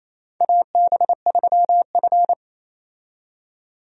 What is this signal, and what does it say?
Morse code, heard as a keyed tone, about 28 wpm, 690 Hz. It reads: AB3F
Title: USA Call Signs - 4 Characters - 28wpm